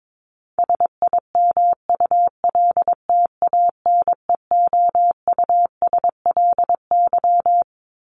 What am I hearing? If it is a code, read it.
Morse code, heard as a keyed tone, about 22 wpm, 700 Hz. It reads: SIMULTANEOUSLY
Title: LY Suffix - Top 100 Words with Base Words 22wpm